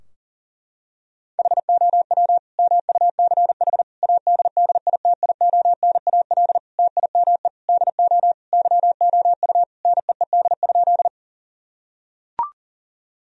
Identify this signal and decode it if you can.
Morse code, heard as a keyed tone, about 40 wpm, 700 Hz. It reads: HOW MUCH ADDITIONAL TIME DO YOU NEED?
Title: Sentences from Top 1000 Words - Review Part 4 - 40wpm